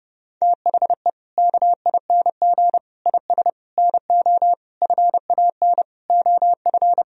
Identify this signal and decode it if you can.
Morse code, heard as a keyed tone, about 30 wpm, 700 Hz. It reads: THE KING IS NO FAN OF
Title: Sentences from Top 1000 Words - Review Part 4 - 30wpm